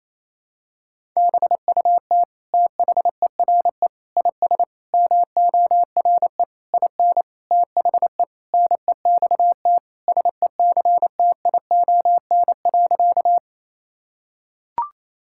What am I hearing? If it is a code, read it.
Morse code, heard as a keyed tone, about 28 wpm, 700 Hz. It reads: BUT THERE IS MORE IN THE NEXT SECTION.
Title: Sentences from Top 1000 Words - Review Part 5 - 28wpm